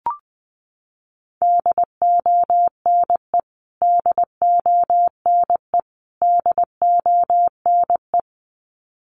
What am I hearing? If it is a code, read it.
Morse code, heard as a keyed tone, about 20 wpm, 700 Hz. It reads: DONE DONE DONE
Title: Top 500 Words Repeated 20wpm